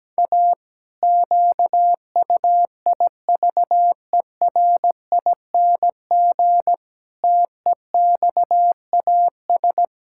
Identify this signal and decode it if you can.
Morse code, heard as a keyed tone, about 17 wpm, 700 Hz. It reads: A QUIVERING TEXAS
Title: Pangram Challenge v2 - 17wpm